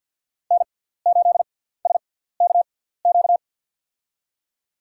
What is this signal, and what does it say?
Morse code, heard as a keyed tone, about 47 wpm, spaced as for 17 wpm, 700 Hz. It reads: N8SKQ
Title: USA Call Signs - 6x Character Spacing - 50wpm